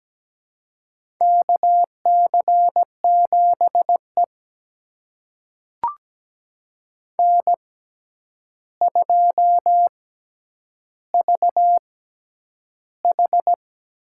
Morse code, heard as a keyed tone, about 17 wpm, spaced as for 9 wpm, 700 Hz. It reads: KC7E N2VH
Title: USA Call Signs - 4 Characters - 6x Character Spacing - 17wpm